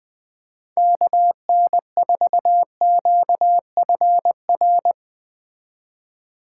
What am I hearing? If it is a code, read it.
Morse code, heard as a keyed tone, about 20 wpm, 700 Hz. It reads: KN4QFR
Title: Speed Racing - USA Call Signs 17wpm